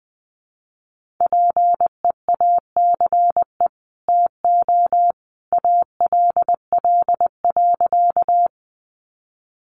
Morse code, heard as a keyed tone, about 20 wpm, 700 Hz. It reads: PEACE TO ALL.
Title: Sentences from Top 1500 Words 20wpm